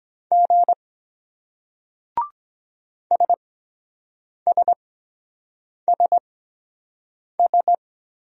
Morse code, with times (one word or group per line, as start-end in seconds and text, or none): 0.3-0.7 G
3.1-7.7 SSSS